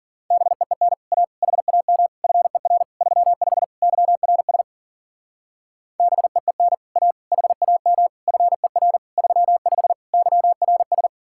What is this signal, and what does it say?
Morse code, heard as a keyed tone, about 52 wpm, 700 Hz. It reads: BEEN A HAM FER 35 YRS BEEN A HAM FER 35 YRS